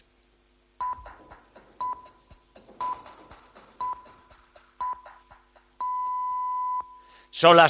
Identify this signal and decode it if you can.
Radio time signal (time pips). It is heard from 0.8 to 6.8 s.